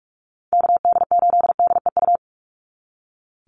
Morse code, heard as a keyed tone, about 44 wpm, 700 Hz. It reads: KD8DEU
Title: NA SOTA Chaser Callsigns 45wpm